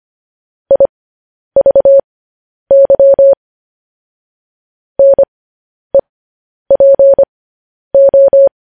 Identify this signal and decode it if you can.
Morse code, heard as a keyed tone, about 25 wpm, spaced as for 10 wpm, 550 Hz. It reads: IVY NEPO
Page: news-headlines-in-morse-code-at-10-wpm